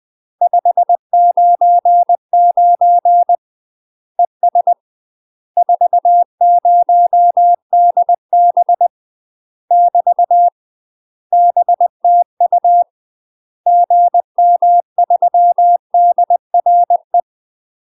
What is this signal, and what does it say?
Morse code, heard as a keyed tone, about 20 wpm, 700 Hz. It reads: 599 ES 40DB <BT> BTU GM3DRE
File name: UK MASTER QSOs TRAINING / 20WPM / UK MASTER QSOs 0001